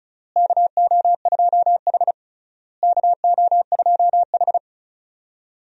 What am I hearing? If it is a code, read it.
Morse code, heard as a keyed tone, about 36 wpm, 700 Hz. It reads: KO2H KO2H
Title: USA Call Signs - 4 Characters - Repeated 3x - 35wpm